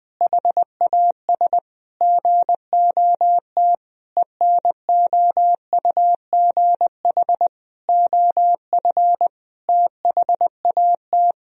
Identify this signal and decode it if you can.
Morse code, heard as a keyed tone, about 20 wpm, 700 Hz. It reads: HAS GOT ENOUGH OF THAT
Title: Sentences from Top 1000 Words - Review Part 3 - 20wpm